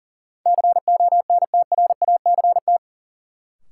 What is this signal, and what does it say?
Morse code, heard as a keyed tone, about 40 wpm, 700 Hz. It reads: CONTRACT
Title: Sentences from Top 1400 Words - New Word Drill - 40wpm